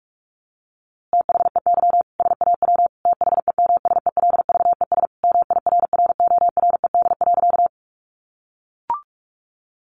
Morse code, heard as a keyed tone, about 44 wpm, 700 Hz. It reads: THEY SAW THEMSELVES MIRRORED.
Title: A Sound Of Thunder 45wpm